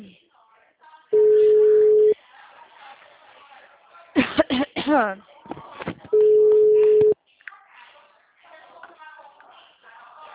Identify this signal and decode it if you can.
Ringback tone (Europe, 425 Hz), with speech.